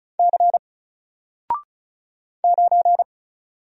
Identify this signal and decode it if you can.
Morse code, heard as a keyed tone, about 34 wpm, 700 Hz. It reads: C 9
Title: Single Letter-Number Alt 35wpm